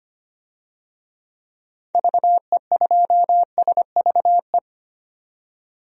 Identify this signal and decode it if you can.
Morse code, heard as a keyed tone, about 25 wpm, 700 Hz. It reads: VE2SVE
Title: DXCC-100 Call Signs - 6 Characters - 25wpm